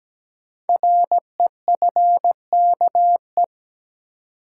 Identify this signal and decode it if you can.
Morse code, heard as a keyed tone, about 17 wpm, 700 Hz. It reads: REFKE
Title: Code Groups - 5 Letter:Numbers - 17wpm